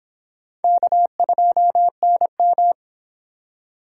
Morse code, heard as a keyed tone, about 29 wpm, 700 Hz. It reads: K2NM